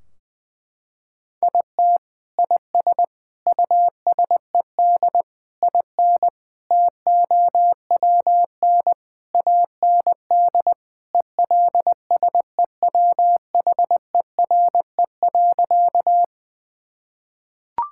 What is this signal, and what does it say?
Morse code, heard as a keyed tone, about 20 wpm, 700 Hz. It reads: IT IS USED IN TOWN AND ELSEWHERE.
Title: Sentences from Top 1600 Words 20wpm